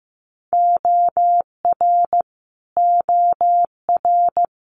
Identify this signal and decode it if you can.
Morse code, heard as a keyed tone, about 16 wpm, 700 Hz. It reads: OR OR
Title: Multi-letter Phonemes Repeated 3x 15wpm